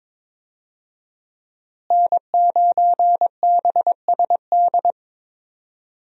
Morse code, heard as a keyed tone, about 22 wpm, 700 Hz. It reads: N9BSD
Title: USA Call Signs - 5 Characters - Repeated 2x - 22wpm